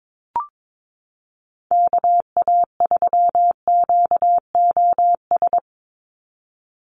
Morse code, heard as a keyed tone, about 22 wpm, 700 Hz. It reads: KA3QOS
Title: USA Call Signs 22wpm